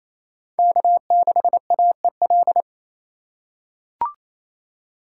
Morse code, heard as a keyed tone, about 28 wpm, 700 Hz. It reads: K6AEL
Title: USA Call Signs - 3x Character Spacing - 28wpm